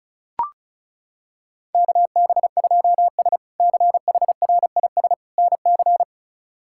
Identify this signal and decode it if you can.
Morse code, heard as a keyed tone, about 36 wpm, 700 Hz. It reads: KB2S CHRIS NC
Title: K1USN SST QSO Practice - 35wpm